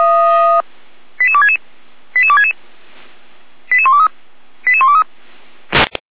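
Five-tone selective call (ZVEI), heard as quick runs of five tones, tones of about 70 ms. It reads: 88266 88266 88123 88123
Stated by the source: ZVEI five-tone call received by SDR, from audio.f85360.feuerwehr-orlamuende.org fire brigade archive.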